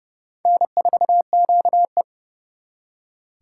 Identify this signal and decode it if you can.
Morse code, heard as a keyed tone, about 37 wpm, 700 Hz. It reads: N4QE
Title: Speed Racing - USA 1x2 Call Signs 30wpm